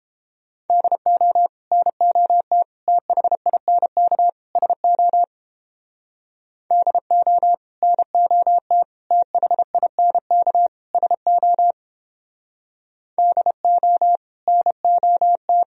Morse code, heard as a keyed tone, about 31 wpm, 700 Hz. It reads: DO NOT THINK SO DO NOT THINK SO DO NOT